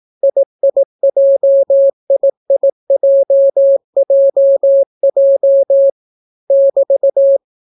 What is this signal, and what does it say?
Morse code, heard as a keyed tone, about 18 wpm, 550 Hz. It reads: IIJIIJJJ <BT>